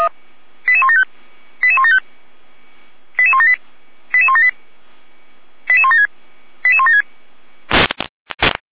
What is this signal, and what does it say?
Five-tone selective call (ZVEI), heard as quick runs of five tones, tones of about 70 ms. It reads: 88175 88175 88178 88178 88176 88176